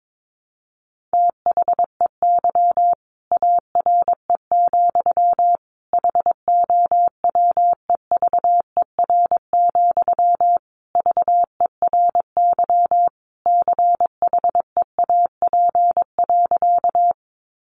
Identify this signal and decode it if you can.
Morse code, heard as a keyed tone, about 22 wpm, 700 Hz. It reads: THEY ARE, HOWEVER, VERY CHEAP.
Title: Sentences from Top 1200 Words 22wpm